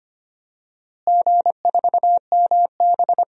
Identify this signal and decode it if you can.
Morse code, heard as a keyed tone, about 25 wpm, 700 Hz. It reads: G4MB
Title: DXCC-100 Call Signs - 4 Characters - Repeated 2x - 25wpm